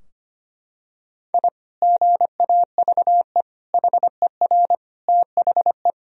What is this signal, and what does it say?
Morse code, heard as a keyed tone, about 25 wpm, 700 Hz. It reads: I GAVE HER THE
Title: Sentences from Top 1000 Words - Review Part 3 - 25wpm